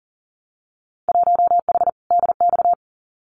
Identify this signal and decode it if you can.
Morse code, heard as a keyed tone, about 42 wpm, 700 Hz. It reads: 1H DX
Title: Winter Field Day Exchanges 40wpm